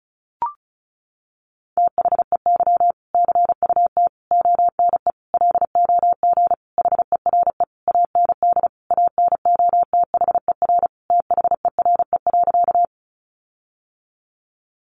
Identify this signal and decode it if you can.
Morse code, heard as a keyed tone, about 34 wpm, 700 Hz. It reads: THEY CUT ONE LOG HERE AND ANOTHER THERE.
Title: Sentences from Top 1400 Words 35wpm